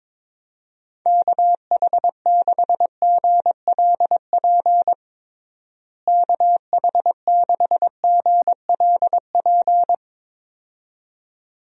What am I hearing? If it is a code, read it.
Morse code, heard as a keyed tone, about 22 wpm, 700 Hz. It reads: KH6GLP KH6GLP
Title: DXCC-100 Call Signs - 6 Characters - Repeated 2x - 22wpm